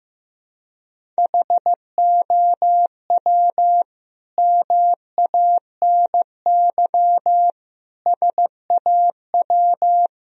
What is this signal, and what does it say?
Morse code, heard as a keyed tone, about 16 wpm, 700 Hz. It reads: HOW MANY SAW
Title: Sentences from Top 1000 Words - Review Part 1 - 15wpm